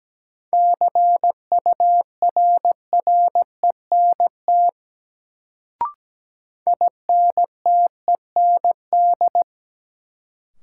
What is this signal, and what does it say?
Morse code, heard as a keyed tone, about 17 wpm, 700 Hz. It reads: CURRENT INTEND
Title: Sentences from Top 1600 Words - New Word Drill Alt - 17wpm